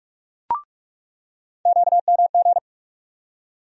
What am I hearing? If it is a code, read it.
Morse code, heard as a keyed tone, about 46 wpm, 700 Hz. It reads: QMG
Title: USA Call Sign Postfixes - 3 Letters 45wpm